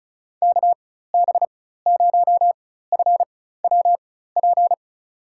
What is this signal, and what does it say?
Morse code, heard as a keyed tone, about 36 wpm, spaced as for 16 wpm, 700 Hz. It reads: KB0FWP
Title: USA Call Signs - 6 Characters - 4x Character Spacing - 35wpm